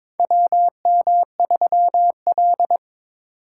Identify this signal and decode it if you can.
Morse code, heard as a keyed tone, about 22 wpm, 700 Hz. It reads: WM3L
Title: USA Call Signs - 4 Characters - 2x Character Spacing - 22wpm